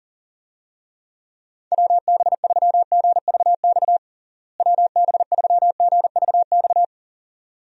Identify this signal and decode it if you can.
Morse code, heard as a keyed tone, about 40 wpm, 700 Hz. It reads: WB3GVX WB3GVX
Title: USA Call Signs - 6 Characters - Repeated 2x - 40wpm